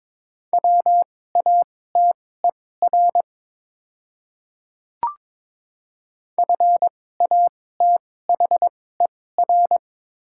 Morse code, heard as a keyed tone, about 21 wpm, spaced as for 16 wpm, 700 Hz. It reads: WATER FATHER
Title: 2x Character Spacing - Top 500 Words - 22wpm